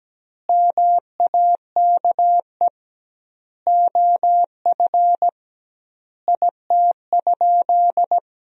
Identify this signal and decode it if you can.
Morse code, heard as a keyed tone, about 17 wpm, 700 Hz. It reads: MAKE OF IT?
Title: 2x Word-Spacing - Sentences from Top 100 Words - 17wpm